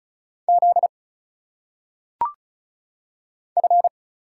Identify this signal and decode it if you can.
Morse code, heard as a keyed tone, about 34 wpm, 700 Hz. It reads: Z F
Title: Single Letters - Phonetic Pronunciation 35wpm